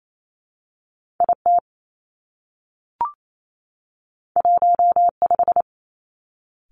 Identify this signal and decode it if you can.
Morse code, heard as a keyed tone, about 28 wpm, 700 Hz. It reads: I T 1 5
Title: Two Challenging Letter-Number Combinations 28wpm